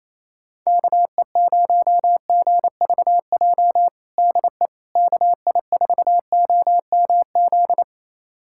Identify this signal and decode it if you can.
Morse code, heard as a keyed tone, about 28 wpm, 700 Hz. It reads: KE0GVJ DE KI4OMZ
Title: USA Callsign DE Callsign 28wpm